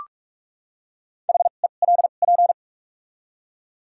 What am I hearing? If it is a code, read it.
Morse code, heard as a keyed tone, about 45 wpm, spaced as for 32 wpm, 700 Hz. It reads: HELP
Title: 2x Character Spacing - Top 200 Words - 45wpm